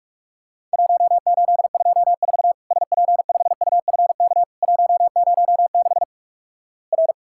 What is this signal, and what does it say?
Morse code, heard as a keyed tone, about 46 wpm, 700 Hz. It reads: 1824 SP5UFK 106 R
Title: Worked All Europe DX Contest - QTC Exchange - 45wpm